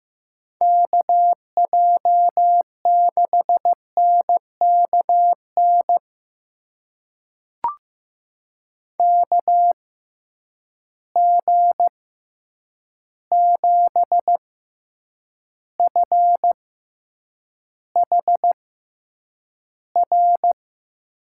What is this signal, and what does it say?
Morse code, heard as a keyed tone, about 15 wpm, spaced as for 8 wpm, 700 Hz. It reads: KJ6NKN KG7FHR
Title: USA Call Signs - 6x Character Spacing - 15wpm